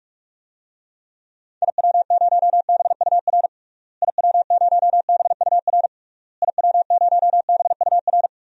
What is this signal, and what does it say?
Morse code, heard as a keyed tone, about 46 wpm, 700 Hz. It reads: IW0BUR IW0BUR IW0BUR
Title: DXCC-100 Call Signs - 6 Characters - Repeated 3x - 45wpm